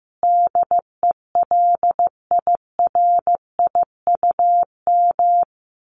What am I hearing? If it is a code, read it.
Morse code, heard as a keyed tone, about 15 wpm, 700 Hz. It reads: DELIRIUM
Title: A Sound Of Thunder 15wpm